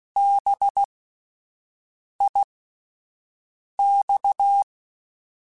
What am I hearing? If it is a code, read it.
Morse code, heard as a keyed tone, about 16 wpm, spaced as for 5 wpm, 800 Hz. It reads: BIX
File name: Spaced Alphas / Spaced Alphas mp3s / Bs sp @10